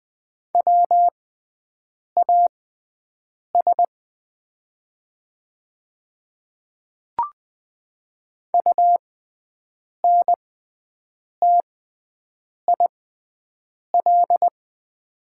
Morse code, heard as a keyed tone, about 20 wpm, spaced as for 7 wpm, 700 Hz. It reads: WAS UNTIL
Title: 6x Character Spacing - Top 300 Words - 20wpm